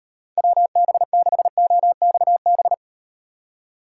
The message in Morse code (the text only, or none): WB6OXB